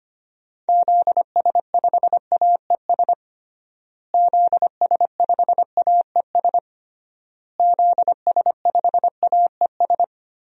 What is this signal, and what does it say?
Morse code, heard as a keyed tone, about 25 wpm, 700 Hz. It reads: ZS5AES ZS5AES ZS5AES
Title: DXCC-100 Call Signs - 6 Characters - Repeated 3x - 25wpm